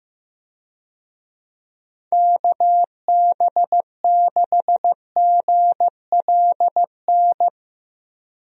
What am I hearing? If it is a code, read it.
Morse code, heard as a keyed tone, about 15 wpm, 700 Hz. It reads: KB6GLN
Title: USA Call Signs - 6 Characters - Repeated 3x - 15wpm